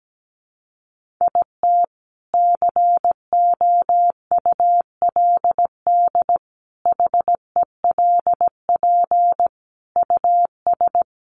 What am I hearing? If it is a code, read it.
Morse code, heard as a keyed tone, about 17 wpm, 700 Hz. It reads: IT COULD HELP US
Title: Sentences from Top 1400 Words 17wpm